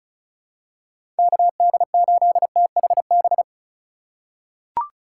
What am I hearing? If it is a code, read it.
Morse code, heard as a keyed tone, about 34 wpm, 700 Hz. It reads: KD8THB
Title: USA Call Signs - 6 Characters - 3x Character Spacing - 35wpm